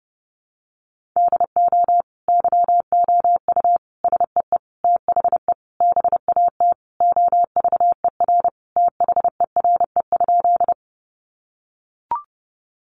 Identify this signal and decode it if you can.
Morse code, heard as a keyed tone, about 30 wpm, 700 Hz. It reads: DO YOU SEE THE BAT OVER THERE?
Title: Sentences from Top 1500 Words 30wpm